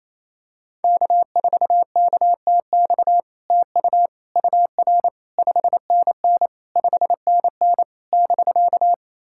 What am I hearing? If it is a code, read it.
Morse code, heard as a keyed tone, about 28 wpm, 700 Hz. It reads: K4KTX TU UR 5NN 5NN <BK>
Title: SOTA QSO Practice 28wpm